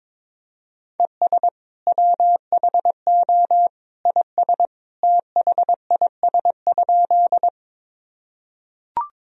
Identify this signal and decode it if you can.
Morse code, heard as a keyed tone, about 21 wpm, 700 Hz. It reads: ES WHO IS THIS?
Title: Sentences from Top 100 Words and Abbreviations 22wpm